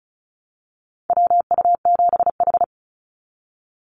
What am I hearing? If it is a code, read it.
Morse code, heard as a keyed tone, about 36 wpm, 700 Hz. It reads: WU7H
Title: NA SOTA Activator Callsigns 35wpm